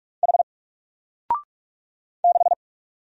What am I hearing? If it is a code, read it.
Morse code, heard as a keyed tone, about 46 wpm, 700 Hz. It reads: H 6
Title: Single Character (Letter:Number:Punctuation:Prosign) 45wpm